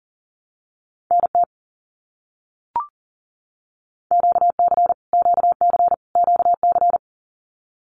Morse code, heard as a keyed tone, about 38 wpm, 700 Hz. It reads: NT QC QC QC
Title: Canadian Provinces and Territories - Repeated 3x - 40wpm